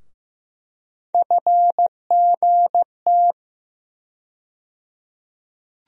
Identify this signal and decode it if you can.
Morse code, heard as a keyed tone, about 15 wpm, 700 Hz. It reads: FGT
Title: USA Call Sign Postfixes - 3 Letters 15wpm